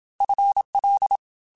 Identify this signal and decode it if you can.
Morse code, heard as a keyed tone, about 27 wpm, 800 Hz. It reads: FL
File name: FL / FL mp3s / FL Prefixes 3x2 Twice @25